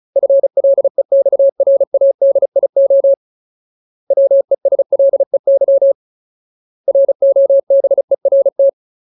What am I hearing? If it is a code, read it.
Morse code, heard as a keyed tone, about 34 wpm, 550 Hz. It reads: FLEXRADIO WESLEY ROBERT
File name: Ultimate Word List / 35WPM Ultimate